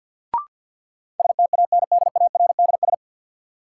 Morse code, heard as a keyed tone, about 47 wpm, 700 Hz. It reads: STANDARDS
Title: Sentences from Top 1000 Words - Review Part 6 - New Word Drill - 50wpm